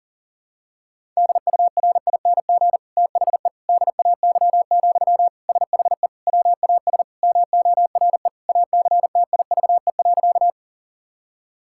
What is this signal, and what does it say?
Morse code, heard as a keyed tone, about 40 wpm, 700 Hz. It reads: DURING THE DAY, SHE WAS MORE ACTIVE.
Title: Sentences from Top 1000 Words - Review Part 6 - 40wpm